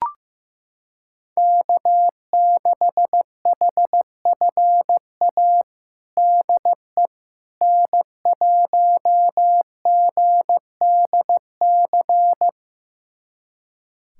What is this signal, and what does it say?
Morse code, heard as a keyed tone, about 15 wpm, 700 Hz. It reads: K6HFA DE N1GDC
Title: USA Callsign DE Callsign - 5 Characters - 15wpm